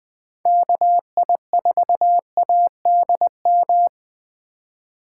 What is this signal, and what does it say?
Morse code, heard as a keyed tone, about 20 wpm, 700 Hz. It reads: KI4ADM
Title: Speed Racing - USA 2x3 Call Signs 17wpm